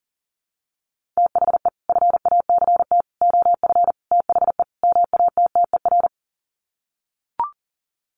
Morse code, heard as a keyed tone, about 40 wpm, 700 Hz. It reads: THE FACT OF THE MATTER
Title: Sets of 5 Words 40wpm